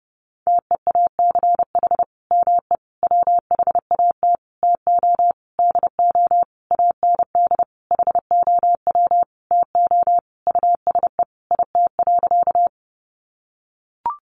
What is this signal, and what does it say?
Morse code, heard as a keyed tone, about 30 wpm, 700 Hz. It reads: TEACH ME WHAT TO DO AND HOW TO USE IT.
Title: Sentences from Top 500 Words 30wpm